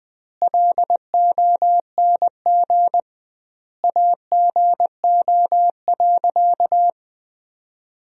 Morse code, heard as a keyed tone, about 21 wpm, 700 Hz. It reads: LONG AGO.
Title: 2x Word-Spacing - Sentences from Top 500 Words - 20wpm